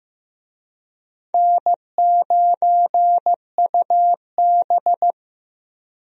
Morse code, heard as a keyed tone, about 15 wpm, 700 Hz. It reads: N9UB
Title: USA Call Signs - 4 Characters - Repeated 3x - 15wpm